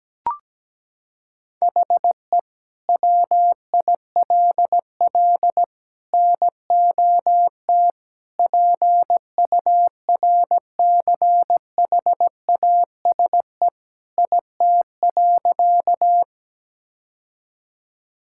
Morse code, heard as a keyed tone, about 17 wpm, 700 Hz. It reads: HE WILL NOT PURCHASE IT.
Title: Sentences from Top 1000 Words - Review Part 5 - 17wpm